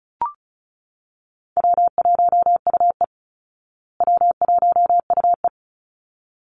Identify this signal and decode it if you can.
Morse code, heard as a keyed tone, about 34 wpm, 700 Hz. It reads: W1UE W1UE
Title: Call Signs Repeated 3x - 35wpm